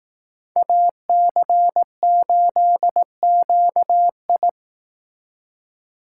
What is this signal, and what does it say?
Morse code, heard as a keyed tone, about 18 wpm, 700 Hz. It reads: AC8QI
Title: Speed Racing - USA 2x2 Call Signs 15wpm